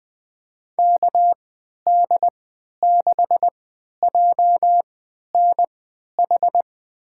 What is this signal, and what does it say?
Morse code, heard as a keyed tone, about 20 wpm, spaced as for 11 wpm, 700 Hz. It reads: KD6JNH
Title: USA Call Signs - 6 Characters - 3x Character Spacing - 20wpm